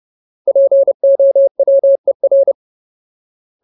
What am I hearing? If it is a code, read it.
Morse code, heard as a keyed tone, about 30 wpm, 550 Hz. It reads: POWER
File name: Ultimate Word List / 30WPM Ultimate